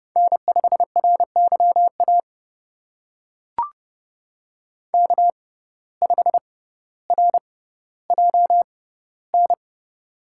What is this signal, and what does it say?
Morse code, heard as a keyed tone, about 30 wpm, spaced as for 15 wpm, 700 Hz. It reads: N5RYA K5RJN
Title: USA Call Signs - 5 Characters - 6x Character Spacing - 30wpm